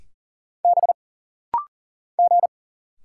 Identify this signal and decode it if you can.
Morse code, heard as a keyed tone, about 40 wpm, 700 Hz. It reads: B G
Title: Single Letter-Number Alt - Rapid-Fire 40wpm